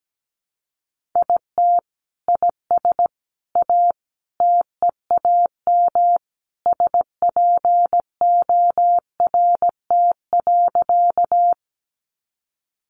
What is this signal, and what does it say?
Morse code, heard as a keyed tone, about 17 wpm, 700 Hz. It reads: IT IS A TEAM SPORT.
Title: Sentences from Top 1500 Words 17wpm